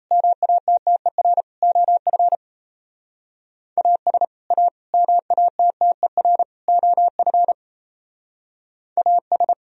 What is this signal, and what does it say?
Morse code, heard as a keyed tone, about 33 wpm, 700 Hz. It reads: MATTER OF AS A MATTER OF AS
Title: Speed-Racing - Sets of 4 Words 28wpm